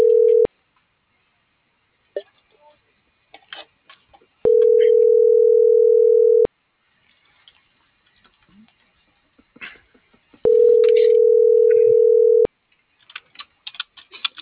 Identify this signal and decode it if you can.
Ringback tone (North America), 440 + 480 Hz.